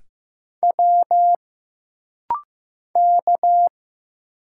Morse code, heard as a keyed tone, about 15 wpm, 700 Hz. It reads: W K
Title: Single Letter-Number Alt - Rapid-Fire 15wpm